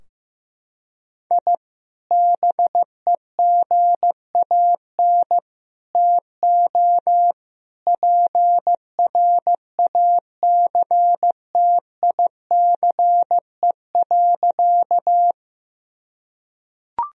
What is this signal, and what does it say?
Morse code, heard as a keyed tone, about 15 wpm, 700 Hz. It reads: I BEGAN TO PRACTICE.
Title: Sentences from Top 1000 Words - Review Part 4 - 15wpm